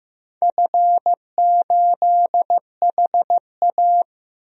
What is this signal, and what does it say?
Morse code, heard as a keyed tone, about 15 wpm, 700 Hz. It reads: F8HA
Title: DXCC-100 Call Signs - 4 Characters - 15wpm